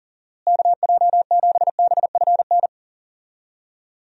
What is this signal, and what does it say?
Morse code, heard as a keyed tone, about 37 wpm, 700 Hz. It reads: KJ7BFN
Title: USA Call Signs - 6 Characters - Repeated 3x - 40wpm